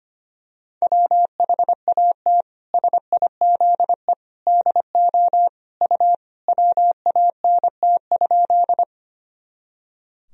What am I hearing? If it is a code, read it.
Morse code, heard as a keyed tone, about 25 wpm, 700 Hz. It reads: WHAT SIZE DO U WANT?
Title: Sentences from Top 400 Words and Abbreviations 25wpm